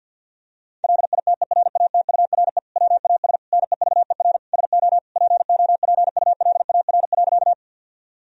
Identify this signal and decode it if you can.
Morse code, heard as a keyed tone, about 50 wpm, 700 Hz. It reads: LITERATURE WAS NEVER SO POPULAR.
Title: Sentences from Top 1000 Words - Review Part 3 - 50wpm